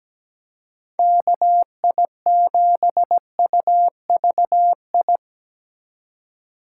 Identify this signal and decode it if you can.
Morse code, heard as a keyed tone, about 19 wpm, 700 Hz. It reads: KI7UVI
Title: Speed Racing - USA Call Signs 17wpm